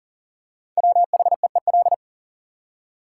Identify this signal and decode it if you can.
Morse code, heard as a keyed tone, about 40 wpm, 700 Hz. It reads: WHEEL
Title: Five Letter Words - 40wpm